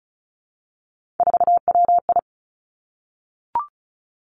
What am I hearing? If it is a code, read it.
Morse code, heard as a keyed tone, about 36 wpm, 700 Hz. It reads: 4WI
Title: Three Challenging Letter-Number Combinations 35wpm